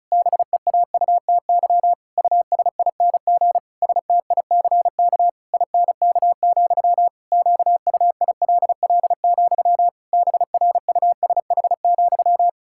Morse code, heard as a keyed tone, about 34 wpm, 700 Hz. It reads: BEAUTY USING STICK INK, QUILL, BRUSH,
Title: Pangram Challenge v2 - 35wpm